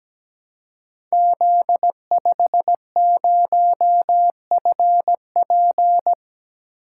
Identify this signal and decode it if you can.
Morse code, heard as a keyed tone, about 17 wpm, 700 Hz. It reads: Z50FP
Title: Code Groups - 5 Letter:Numbers - 17wpm